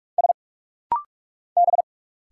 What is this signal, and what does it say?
Morse code, heard as a keyed tone, about 46 wpm, 700 Hz. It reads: S B
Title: Single Letters - Rapid-Fire 45wpm